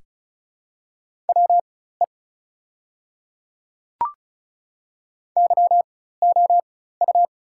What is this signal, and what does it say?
Morse code, heard as a keyed tone, about 36 wpm, spaced as for 16 wpm, 700 Hz. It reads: WE YOU
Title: 4x Character Spacing - Top 100 Words - 35wpm